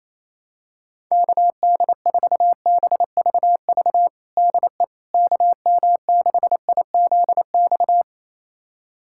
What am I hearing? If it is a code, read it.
Morse code, heard as a keyed tone, about 27 wpm, 700 Hz. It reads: KD4BVV DE KM6IZX
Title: USA Callsign DE Callsign - 6 Characters - 28wpm